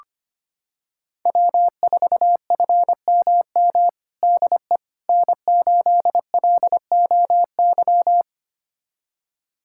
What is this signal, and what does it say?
Morse code, heard as a keyed tone, about 25 wpm, 700 Hz. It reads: W4FMM DE N8LOY
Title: USA Callsign DE Callsign - 5 Characters - 25wpm